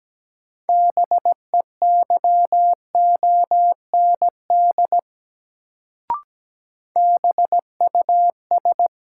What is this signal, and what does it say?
Morse code, heard as a keyed tone, about 17 wpm, 700 Hz. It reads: BEYOND BUS
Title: Sentences from Top 1000 Words - Review Part 6 - New Word Drill - 17wpm